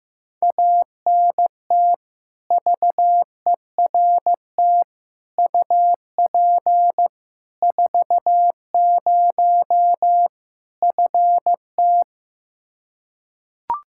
Morse code, heard as a keyed tone, about 15 wpm, 700 Hz. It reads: ANT VERT UP 40 FT
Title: Practice QSO - Exchange 2 - Info - 15wpm